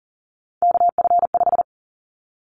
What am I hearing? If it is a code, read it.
Morse code, heard as a keyed tone, about 40 wpm, 700 Hz. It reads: KF5
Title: Callsign Prefixes for Top 25 DXCC - 40wpm